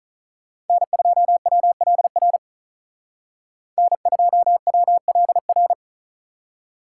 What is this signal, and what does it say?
Morse code, heard as a keyed tone, about 45 wpm, 700 Hz. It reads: N2WLR N2WLR